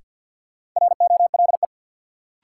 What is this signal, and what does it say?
Morse code, heard as a keyed tone, about 52 wpm, 700 Hz. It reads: ROLE
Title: Sentences from Top 1500 Words - New Word Drill Alt - 50wpm